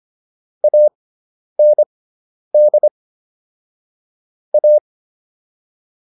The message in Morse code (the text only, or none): AND A